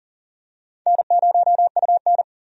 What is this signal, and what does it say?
Morse code, heard as a keyed tone, about 40 wpm, 700 Hz. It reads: N0UN
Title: USA Call Signs - 4 Characters - 40wpm